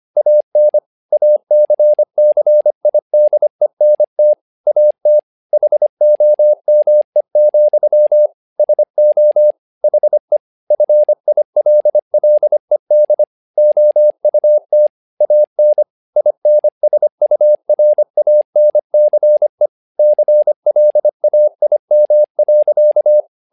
Morse code, heard as a keyed tone, about 25 wpm, 600 Hz. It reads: AN ACCIDENT AT HOME, SO HE FILLED OUT AN INSURANCE CLAIM.
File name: learning center / tower trouble / tower trouble 25wpm